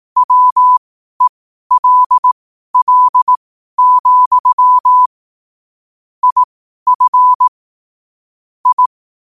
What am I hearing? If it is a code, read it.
Morse code, heard as a keyed tone, about 18 wpm, spaced as for 12 wpm, 1000 Hz. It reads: WELL, IF I
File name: qotdfiles / 12wpm